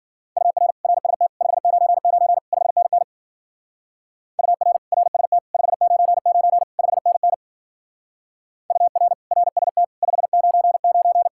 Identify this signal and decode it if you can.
Morse code, heard as a keyed tone, about 50 wpm, 700 Hz. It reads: UR RST 599 5NN UR RST 599 5NN UR RST 599
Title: Speed-Racing Common QSO Phrases - 40wpm